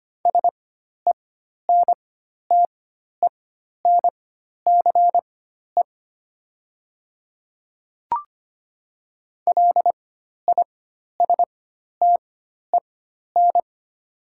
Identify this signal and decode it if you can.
Morse code, heard as a keyed tone, about 25 wpm, spaced as for 12 wpm, 700 Hz. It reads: SENTENCE LISTEN
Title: 4x Character Spacing - Top 500 Words - 25wpm